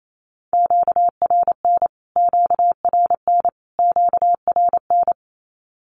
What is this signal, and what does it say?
Morse code, heard as a keyed tone, about 28 wpm, 700 Hz. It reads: QRN QRN QRN
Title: Q-Codes Repeated 3x 28wpm